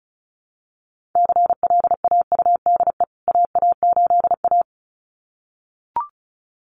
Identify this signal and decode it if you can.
Morse code, heard as a keyed tone, about 34 wpm, 700 Hz. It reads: CLAUDE AA8A
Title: CWT Exchange 35wpm